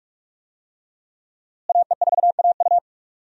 Morse code, heard as a keyed tone, about 46 wpm, 700 Hz. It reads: AE4AU
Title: USA Call Signs - 5 Characters - Repeated 2x - 45wpm